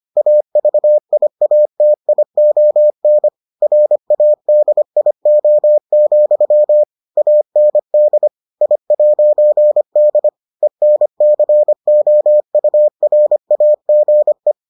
Morse code, heard as a keyed tone, about 25 wpm, 600 Hz. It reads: AVIATION RADIO, AND I'D ENCOURAGE